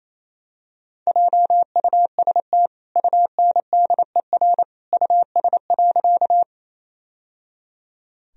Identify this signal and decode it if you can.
Morse code, heard as a keyed tone, about 28 wpm, 700 Hz. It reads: JUST UNDER US.
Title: Sentences from Top 1000 Words - Review Part 1 - 28wpm